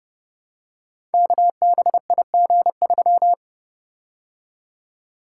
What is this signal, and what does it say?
Morse code, heard as a keyed tone, about 30 wpm, 700 Hz. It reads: KBIG3
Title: Code Groups - 5 Letter:Numbers - 30wpm